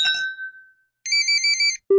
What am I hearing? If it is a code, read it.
There is no code.